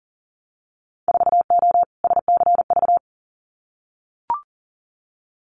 Morse code, heard as a keyed tone, about 40 wpm, 700 Hz. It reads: 4O SCV
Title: Winter Field Day Exchanges 40wpm